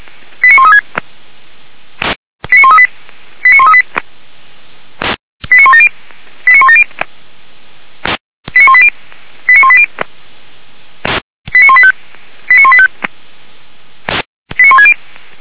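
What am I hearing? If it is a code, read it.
Five-tone selective call (ZVEI), heard as quick runs of five tones, tones of about 80 ms. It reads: 89136 89138 89138 89170 89170 89180 89180 89185 89185 89160
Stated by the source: ZVEI five-tone call received by SDR, logged by Feuerwehr Orlamünde (fire brigade).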